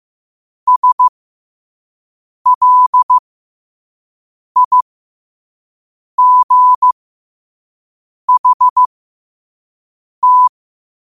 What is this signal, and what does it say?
Morse code, heard as a keyed tone, about 15 wpm, spaced as for 5 wpm, 1000 Hz. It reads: SLIGHT
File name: qotdfiles / 5wpm